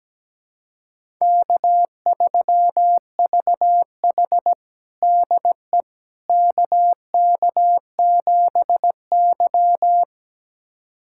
Morse code, heard as a keyed tone, about 17 wpm, 700 Hz. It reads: K3VH DE KK7Y
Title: USA Callsign DE Callsign - 4 Characters - 17wpm